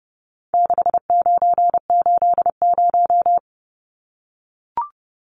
Morse code, heard as a keyed tone, about 30 wpm, 700 Hz. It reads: 6980
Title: Numbers - 4-Digit 30wpm